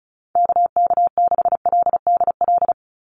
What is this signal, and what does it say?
Morse code, heard as a keyed tone, about 34 wpm, 700 Hz. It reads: KK6LDL
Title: USA Call Signs 35wpm